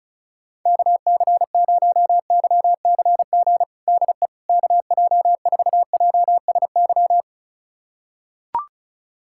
Morse code, heard as a keyed tone, about 34 wpm, 700 Hz. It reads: KC0YCG DE KJ4JSY
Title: USA Callsign DE Callsign - 6 Characters - 35wpm